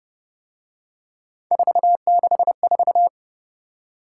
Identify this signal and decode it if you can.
Morse code, heard as a keyed tone, about 29 wpm, 700 Hz. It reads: 464
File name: Speed-Racing - Three Letter-Number Combinations - 25wpm